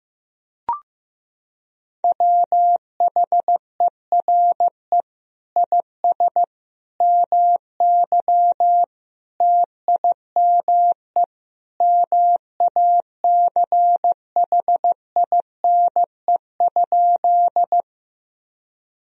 Morse code, heard as a keyed tone, about 15 wpm, 700 Hz. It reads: WHERE IS MY TIME MACHINE?
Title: Sentences from Top 1000 Words - Review Part 2 - 15wpm